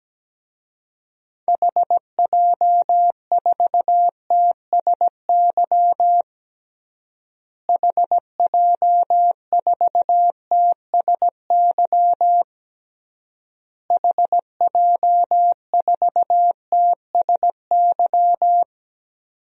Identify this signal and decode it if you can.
Morse code, heard as a keyed tone, about 17 wpm, 700 Hz. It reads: HJ4TSY HJ4TSY HJ4TSY
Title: DXCC-100 Call Signs - 6 Characters - Repeated 3x - 17wpm